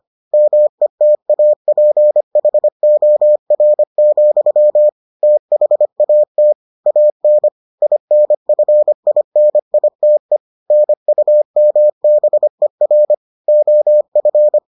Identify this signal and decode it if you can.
Morse code, heard as a keyed tone, about 25 wpm, 600 Hz. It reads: METAPHOR, THAT AN INFINITE NUMBER OF